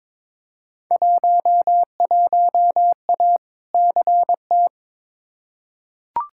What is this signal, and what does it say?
Morse code, heard as a keyed tone, about 22 wpm, 700 Hz. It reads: 11A CT
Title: Field Day Exchanges Encore - 22wpm